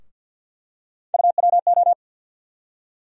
Morse code, heard as a keyed tone, about 49 wpm, 690 Hz. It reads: UWO
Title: USA Call Sign Postfixes - 3 Letters 50wpm